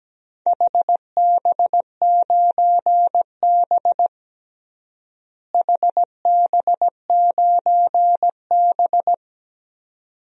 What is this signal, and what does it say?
Morse code, heard as a keyed tone, about 17 wpm, 700 Hz. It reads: HB9B HB9B
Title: DXCC-100 Call Signs - 4 Characters - Repeated 2x - 17wpm